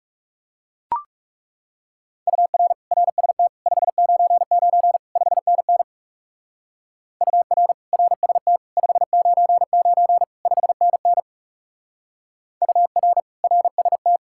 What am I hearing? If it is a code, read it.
Morse code, heard as a keyed tone, about 37 wpm, 700 Hz. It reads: UR RST 599 5NN UR RST 599 5NN UR RST
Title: Speed-Racing Common QSO Phrases Encore - 30wpm